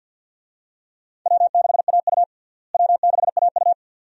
Morse code, heard as a keyed tone, about 50 wpm, 700 Hz. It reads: W6AU W6AU
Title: USA Call Signs - 4 Characters - Repeated 2x - 50wpm